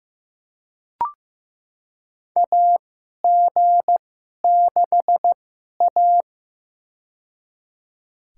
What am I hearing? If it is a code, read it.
Morse code, heard as a keyed tone, about 15 wpm, spaced as for 11 wpm, 700 Hz. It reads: AG6A
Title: USA Call Signs - 4 Characters - 2x Character Spacing - 15wpm